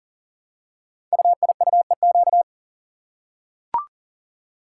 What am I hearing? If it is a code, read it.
Morse code, heard as a keyed tone, about 38 wpm, 700 Hz. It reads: UIUEQ
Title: Code Groups - 5 Letters - 40wpm